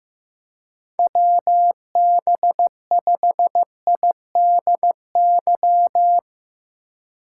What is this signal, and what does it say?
Morse code, heard as a keyed tone, about 15 wpm, 700 Hz. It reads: WB5IDY
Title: K1USN SST QSO Practice - 15wpm